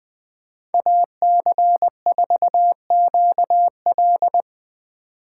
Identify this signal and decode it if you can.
Morse code, heard as a keyed tone, about 20 wpm, 700 Hz. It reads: AC4QL
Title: Speed Racing - USA 2x2 Call Signs 17wpm